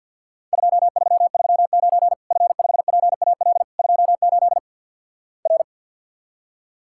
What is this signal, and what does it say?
Morse code, heard as a keyed tone, about 48 wpm, 700 Hz. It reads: 2339 F5PAL 28 R
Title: Worked All Europe DX Contest - QTC Exchange - 50wpm